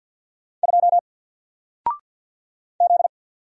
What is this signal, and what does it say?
Morse code, heard as a keyed tone, about 50 wpm, 700 Hz. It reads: 2 Z
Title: Single Character (Letter:Number:Punctuation:Prosign) 50wpm